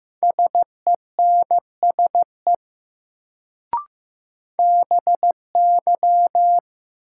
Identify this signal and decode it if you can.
Morse code, heard as a keyed tone, about 15 wpm, 700 Hz. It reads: SENSE BY
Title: Sets of 2 Words - New Word Drill - 15wpm